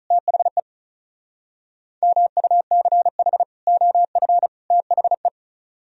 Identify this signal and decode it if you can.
Morse code, heard as a keyed tone, about 44 wpm, 700 Hz. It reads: THE MUCH OF THE